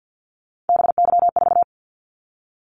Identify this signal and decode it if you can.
Morse code, heard as a keyed tone, about 50 wpm, 700 Hz. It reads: BY4
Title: Callsign Prefixes for Top 100 DXCC - 50wpm